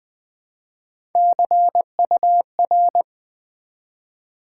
Morse code, heard as a keyed tone, about 20 wpm, 700 Hz. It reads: CUR
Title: Build A Word - 1 Letter At A Time - 3 to 5-Letter Words - 20wpm